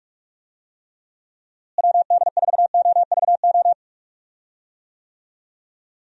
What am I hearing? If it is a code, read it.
Morse code, heard as a keyed tone, about 46 wpm, 700 Hz. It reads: WD4OVO
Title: USA Call Signs - 6 Characters - Repeated 2x - 45wpm